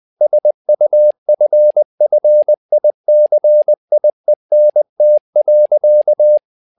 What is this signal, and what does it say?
Morse code, heard as a keyed tone, about 20 wpm, 600 Hz. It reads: SUFFICIENT.
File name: tower trouble 20wpm